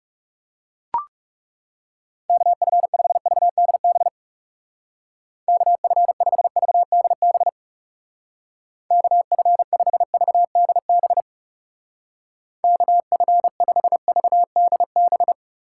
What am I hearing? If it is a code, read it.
Morse code, heard as a keyed tone, about 38 wpm, 700 Hz. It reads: KF5VDB KF5VDB KF5VDB KF5VDB